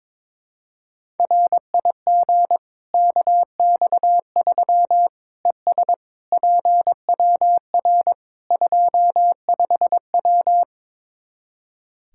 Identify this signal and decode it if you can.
Morse code, heard as a keyed tone, about 22 wpm, 700 Hz. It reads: RIG KX3 ES PWR 25W
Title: Practice QSO - Exchange 2 - Info - 22wpm